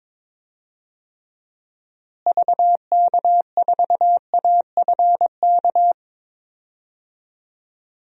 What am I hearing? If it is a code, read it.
Morse code, heard as a keyed tone, about 22 wpm, 700 Hz. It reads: VK4AFK
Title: DXCC-100 Call Signs - 6 Characters - Repeated 3x - 22wpm